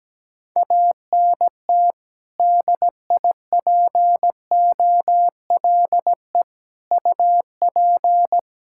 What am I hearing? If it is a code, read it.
Morse code, heard as a keyed tone, about 17 wpm, 700 Hz. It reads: ANT DIPOLE UP
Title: K1USN SST QSO Practice 17wpm